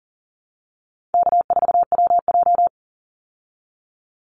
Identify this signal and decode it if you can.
Morse code, heard as a keyed tone, about 40 wpm, 700 Hz. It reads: K4WJ
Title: Easy Call Signs Repeated 3x - 40wpm